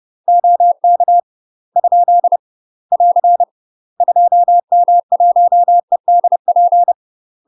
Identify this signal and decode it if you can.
Morse code, heard as a keyed tone, about 30 wpm, 700 Hz. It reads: OK ? <AR> 2M1EDP
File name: UK MASTER QSOs TRAINING / 30WPM / UK MASTER QSOs 0007